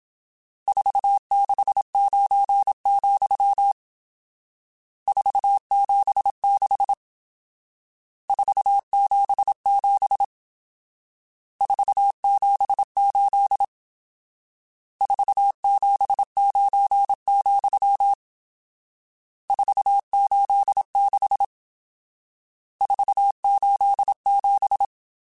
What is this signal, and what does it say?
Morse code, heard as a keyed tone, about 27 wpm, 800 Hz. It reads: 469, 476 477 478 479, 486 487
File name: RSTs / RST-mp3s / Various RST @25